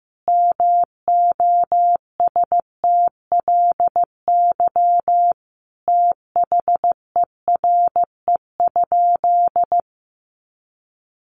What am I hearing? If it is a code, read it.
Morse code, heard as a keyed tone, about 15 wpm, 700 Hz. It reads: MOSTLY THERE?
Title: Sentences from Top 1300 Words 15wpm